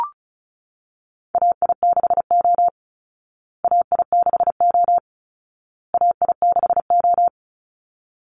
Morse code, heard as a keyed tone, about 34 wpm, 700 Hz. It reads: AI6O AI6O AI6O
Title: Call Signs Repeated 2x - 35wpm